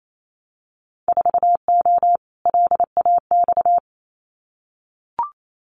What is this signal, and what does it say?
Morse code, heard as a keyed tone, about 28 wpm, 700 Hz. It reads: 4O LAX
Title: Winter Field Day Exchanges Encore 28wpm